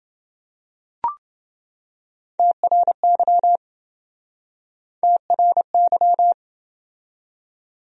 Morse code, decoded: TRY TRY

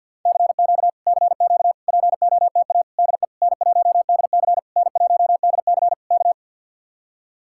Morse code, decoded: CQ CQ POTA DE N1DC N1DC K

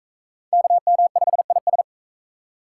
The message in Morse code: KM5IS

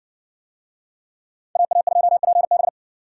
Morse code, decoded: AA2PB

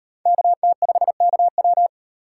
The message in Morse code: KT5KW

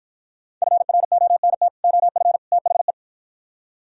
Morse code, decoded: FRONT OF THE

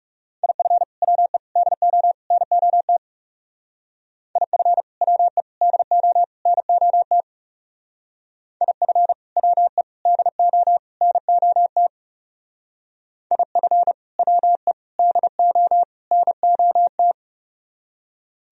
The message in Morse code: IF WE DO NOT IF WE DO NOT IF WE DO NOT IF WE DO NOT

